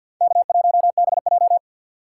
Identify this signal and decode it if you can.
Morse code, heard as a keyed tone, about 49 wpm, 700 Hz. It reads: K1BJ
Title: USA Call Signs - 3x Character Spacing - 50wpm